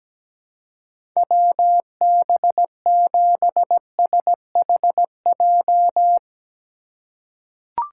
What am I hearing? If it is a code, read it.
Morse code, heard as a keyed tone, about 17 wpm, 700 Hz. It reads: WB7SHJ
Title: USA Call Signs - 2x Character Spacing - 17wpm